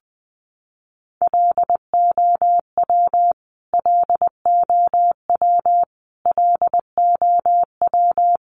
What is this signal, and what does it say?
Morse code, heard as a keyed tone, about 20 wpm, 700 Hz. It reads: LOW LOW LOW
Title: Top 200 Words Repeated 20wpm